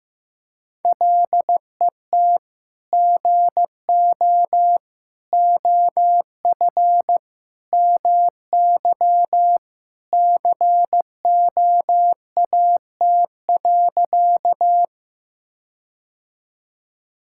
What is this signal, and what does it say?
Morse code, heard as a keyed tone, about 15 wpm, 700 Hz. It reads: LET GO OF MY COAT.
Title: Sentences from Top 1600 Words 15wpm